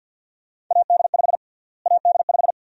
Morse code, heard as a keyed tone, about 47 wpm, 700 Hz. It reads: AD5 AD5
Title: USA Call Sign Prefixes - Repeated 2x - 50wpm